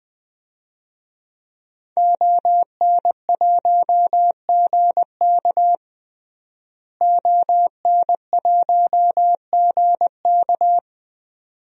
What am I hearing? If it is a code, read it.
Morse code, heard as a keyed tone, about 20 wpm, 700 Hz. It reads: ON1GK ON1GK
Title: DXCC-100 Call Signs - 5 Characters - Repeated 2x - 20wpm